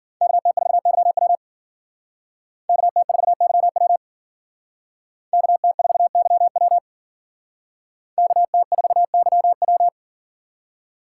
Morse code, heard as a keyed tone, about 51 wpm, 700 Hz. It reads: KT4YW KT4YW KT4YW KT4YW